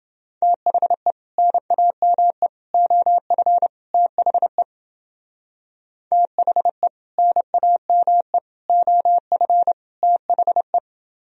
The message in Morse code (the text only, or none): THE NAME OF THE THE NAME OF THE